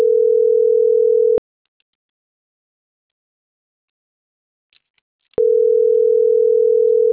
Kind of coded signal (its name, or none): telephone call progress tone